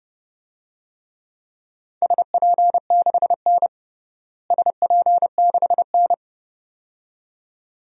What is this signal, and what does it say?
Morse code, heard as a keyed tone, about 30 wpm, 700 Hz. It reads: SP6N SP6N
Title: DXCC-100 Call Signs - 4 Characters - Repeated 2x - 30wpm